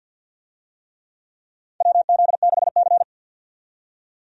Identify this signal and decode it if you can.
Morse code, heard as a keyed tone, about 52 wpm, 700 Hz. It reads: WZ6C
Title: USA Call Signs - 4 Characters - 50wpm